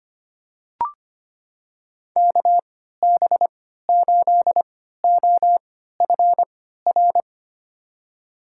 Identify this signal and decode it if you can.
Morse code, heard as a keyed tone, about 25 wpm, spaced as for 14 wpm, 700 Hz. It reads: KB8OFR
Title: USA Call Signs - 6 Characters - 3x Character Spacing - 25wpm